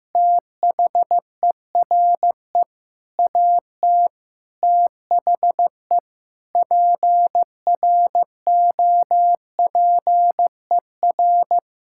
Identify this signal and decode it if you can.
Morse code, heard as a keyed tone, about 15 wpm, 700 Hz. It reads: THERE AT THE PROPER
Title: Sentences from Top 1000 Words - Review Part 5 - 15wpm